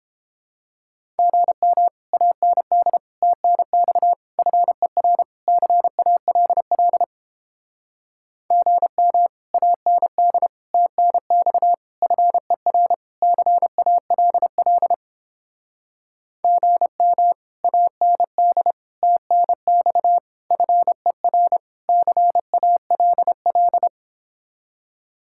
Morse code, heard as a keyed tone, about 29 wpm, 700 Hz. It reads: GM AND TNX FER CALL GM AND TNX FER CALL GM AND TNX FER CALL